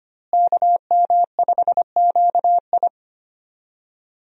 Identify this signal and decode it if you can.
Morse code, heard as a keyed tone, about 25 wpm, 700 Hz. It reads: KM5QI
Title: USA Call Signs - 5 Characters - Repeated 3x - 25wpm